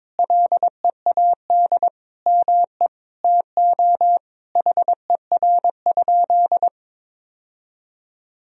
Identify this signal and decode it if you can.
Morse code, heard as a keyed tone, about 22 wpm, 700 Hz. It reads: LEAD ME TO HER?
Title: Sentences from Top 1000 Words - Review Part 2 - 22wpm